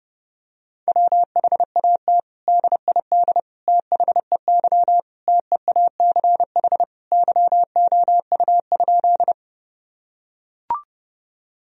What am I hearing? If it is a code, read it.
Morse code, heard as a keyed tone, about 30 wpm, 700 Hz. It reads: WHAT DID THEY TEACH YOU?
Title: Sentences from Top 1000 Words - Review Part 2 - 30wpm